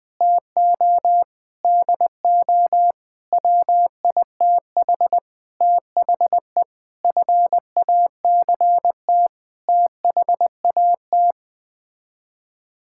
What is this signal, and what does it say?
Morse code, heard as a keyed tone, about 21 wpm, 700 Hz. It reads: TO DO WITH THE FACT THAT
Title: Sets of 8 Words - 20wpm